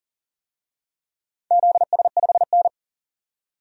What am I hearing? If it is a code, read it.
Morse code, heard as a keyed tone, about 38 wpm, 700 Hz. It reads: ZS5N
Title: DXCC-100 Call Signs - 4 Characters - 40wpm